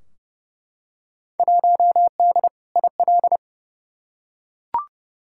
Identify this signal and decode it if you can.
Morse code, heard as a keyed tone, about 30 wpm, 700 Hz. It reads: 1D IL
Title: Field Day Exchanges - 30wpm